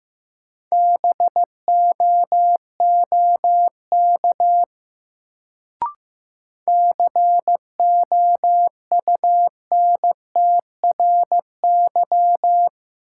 Morse code, heard as a keyed tone, about 15 wpm, 700 Hz. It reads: BOOK COUNTRY
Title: Sentences from Top 300 Words - New Word Drill - 15wpm